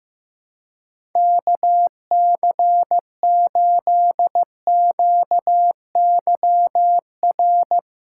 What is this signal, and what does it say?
Morse code, heard as a keyed tone, about 15 wpm, 700 Hz. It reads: KC8QYR
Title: USA Callsign DE Callsign - 6 Characters - 15wpm